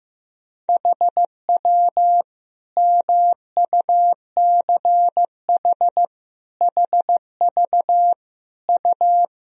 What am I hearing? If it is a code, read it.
Morse code, heard as a keyed tone, about 15 wpm, 700 Hz. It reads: HW MUCH HV U
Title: Sentences from Top 300 Words and Abbreviations 15wpm